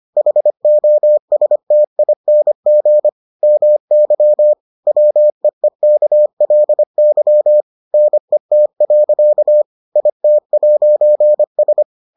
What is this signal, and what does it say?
Morse code, heard as a keyed tone, about 25 wpm, 600 Hz. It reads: HOSTING MY WEEKLY NET. IT'S